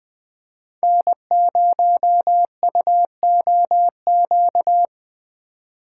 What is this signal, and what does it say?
Morse code, heard as a keyed tone, about 20 wpm, 700 Hz. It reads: N0UOQ
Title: USA Call Signs - 5 Characters - 4x Character Spacing - 20wpm